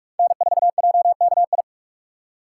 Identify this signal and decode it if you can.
Morse code, heard as a keyed tone, about 46 wpm, 700 Hz. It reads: N4JKI
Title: USA Call Signs - 5 Characters - 2x Character Spacing - 45wpm